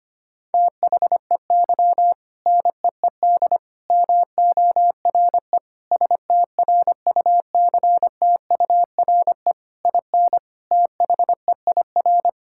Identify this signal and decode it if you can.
Morse code, heard as a keyed tone, about 25 wpm, 700 Hz. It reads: THEY NEED MORE STRUCTURE IN THEIR
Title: Sentences from Top 1000 Words - Review Part 6 - 25wpm